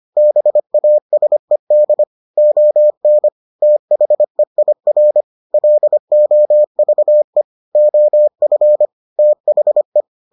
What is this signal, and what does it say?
Morse code, heard as a keyed tone, about 25 wpm, 600 Hz. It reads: BASED ON THEIR LOVE OF THE